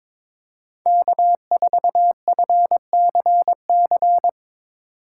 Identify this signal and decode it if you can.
Morse code, heard as a keyed tone, about 22 wpm, 700 Hz. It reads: K4FCC
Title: K1USN SST QSO Practice - 22wpm